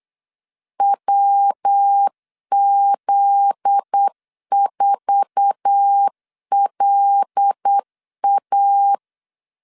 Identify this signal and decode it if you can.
Morse code, heard as a keyed tone, about 8 wpm, 800 Hz. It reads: WZ4LA
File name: LEZIONI / 8 YJ45 / 8 YJ45 8wpm 3214